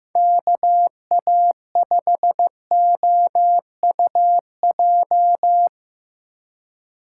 Morse code, heard as a keyed tone, about 15 wpm, 700 Hz. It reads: KA5OUJ
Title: USA Call Signs - 6 Characters - 6x Character Spacing - 15wpm